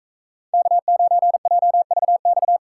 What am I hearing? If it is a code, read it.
Morse code, heard as a keyed tone, about 39 wpm, 700 Hz. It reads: K9JVX